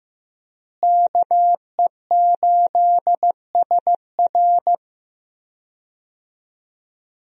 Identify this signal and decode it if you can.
Morse code, heard as a keyed tone, about 14 wpm, 700 Hz. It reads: KE8SR
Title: USA Call Signs - 5 Characters - Repeated 3x - 15wpm